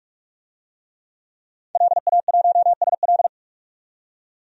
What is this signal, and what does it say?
Morse code, heard as a keyed tone, about 46 wpm, 700 Hz. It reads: LA1SL
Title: DXCC-100 Call Signs - 5 Characters - 45wpm